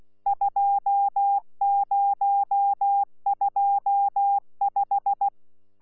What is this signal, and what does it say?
Morse code, heard as a keyed tone, about 16 wpm, 800 Hz. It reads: 2025
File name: Z PI4VRZ / PI4VRZ 14-jun-2025